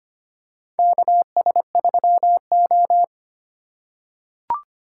Morse code, heard as a keyed tone, about 25 wpm, 700 Hz. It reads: KS3O
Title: USA Call Signs - 4 Characters - 4x Character Spacing - 25wpm